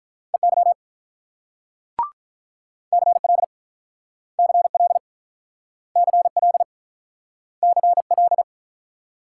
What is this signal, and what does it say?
Morse code, heard as a keyed tone, about 48 wpm, 700 Hz. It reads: EY CL CL CL CL